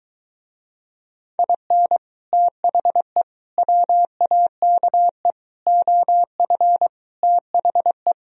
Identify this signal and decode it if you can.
Morse code, heard as a keyed tone, about 23 wpm, 700 Hz. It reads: IN THE WAKE OF THE